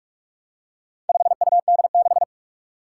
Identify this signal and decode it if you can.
Morse code, heard as a keyed tone, about 44 wpm, 700 Hz. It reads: 5UD6
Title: Four Challenging Letter-Number Combinations 45wpm